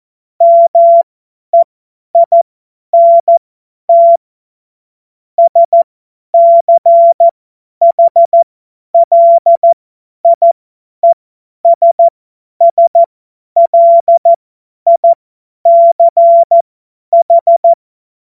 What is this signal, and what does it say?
Morse code, heard as a keyed tone, about 14 wpm, spaced as for 10 wpm, 680 Hz. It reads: MEINT SCHLIESSLICH
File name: Morsekurs / Witze / Witz 12 März 2024 50 F70